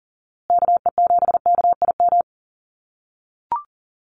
Morse code, heard as a keyed tone, about 40 wpm, 700 Hz. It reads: KE7KIM
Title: USA Call Signs Repeated 3x 40wpm